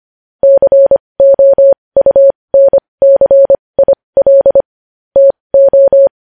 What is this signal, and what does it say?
Morse code, heard as a keyed tone, about 25 wpm, spaced as for 20 wpm, 550 Hz. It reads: COUNCIL TO
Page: news-headlines-in-morse-code-at-20-wpm